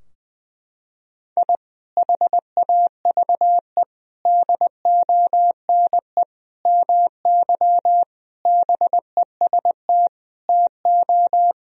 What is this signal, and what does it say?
Morse code, heard as a keyed tone, about 21 wpm, 700 Hz. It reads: I HAVE DONE MY BEST TO
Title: Sentences from Top 1000 Words - Review Part 4 - 20wpm